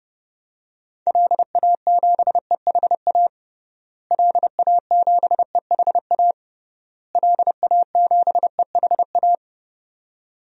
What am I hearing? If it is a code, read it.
Morse code, heard as a keyed tone, about 30 wpm, 700 Hz. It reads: LA7EHA LA7EHA LA7EHA
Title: DXCC-100 Call Signs - 6 Characters - Repeated 3x - 30wpm